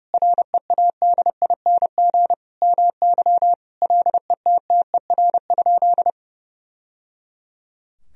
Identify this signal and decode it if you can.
Morse code, heard as a keyed tone, about 32 wpm, 700 Hz. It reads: READING MY LETTER?
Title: Sentences from Top 1000 Words - Review Part 3 - 30wpm